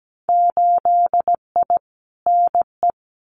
Morse code, heard as a keyed tone, about 17 wpm, 700 Hz. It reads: 8I NE
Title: Winter Field Day Exchanges Encore 17wpm